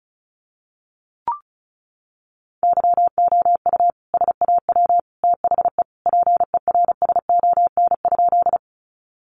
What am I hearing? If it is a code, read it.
Morse code, heard as a keyed tone, about 34 wpm, 700 Hz. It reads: YOU SAW THE PERSON?
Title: Sentences from Top 400 Words 35wpm